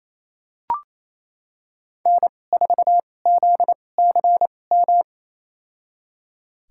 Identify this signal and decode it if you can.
Morse code, heard as a keyed tone, about 28 wpm, spaced as for 20 wpm, 700 Hz. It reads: N4ZCM
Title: USA Call Signs - 2x Character Spacing - 28wpm